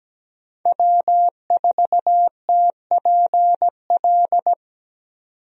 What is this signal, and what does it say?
Morse code, heard as a keyed tone, about 17 wpm, 700 Hz. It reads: W4TPL
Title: USA Call Signs - 5 Characters - 17wpm